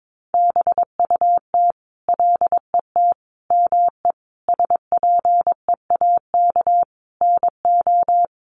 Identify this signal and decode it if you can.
Morse code, heard as a keyed tone, about 22 wpm, 700 Hz. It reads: BUT LET ME SPEAK NO
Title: Sentences from Top 1100 Words 22wpm